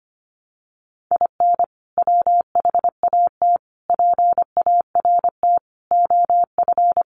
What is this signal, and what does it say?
Morse code, heard as a keyed tone, about 25 wpm, 700 Hz. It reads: IN WHAT PART OF
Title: Sentences from Top 200 Words 25wpm